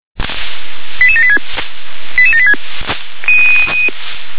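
Five-tone selective call (ZVEI), heard as quick runs of five tones, tones of about 70 ms. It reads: 88786 88786
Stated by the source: ZVEI five-tone selective call received on SDR.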